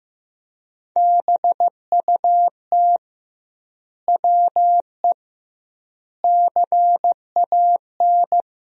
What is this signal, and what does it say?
Morse code, heard as a keyed tone, about 15 wpm, 700 Hz. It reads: BUT WE CAN
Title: 2x Word-Spacing - Sentences from Top 100 Words - 15wpm